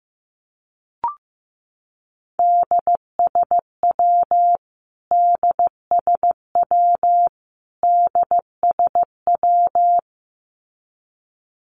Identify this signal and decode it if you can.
Morse code, heard as a keyed tone, about 15 wpm, 700 Hz. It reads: DSW DSW DSW
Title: Abbreviations Repeated 3x - 15wpm